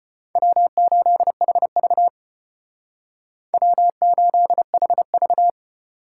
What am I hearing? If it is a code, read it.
Morse code, heard as a keyed tone, about 31 wpm, 700 Hz. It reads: W8HV W8HV
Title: Speed Racing - USA Call Signs 25wpm